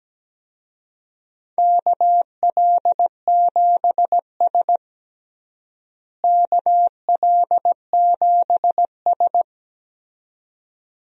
Morse code, heard as a keyed tone, about 17 wpm, 700 Hz. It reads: KL7S KL7S
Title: DXCC-100 Call Signs - 4 Characters - Repeated 2x - 17wpm